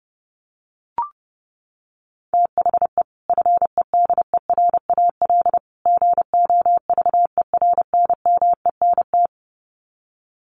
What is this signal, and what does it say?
Morse code, heard as a keyed tone, about 28 wpm, 700 Hz. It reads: THE FEDERAL GOVERNMENT
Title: Sets of 3 Words 30wpm